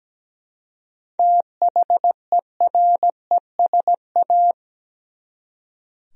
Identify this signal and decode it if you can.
Morse code, heard as a keyed tone, about 17 wpm, 700 Hz. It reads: THERESA
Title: USA Common Names - 17wpm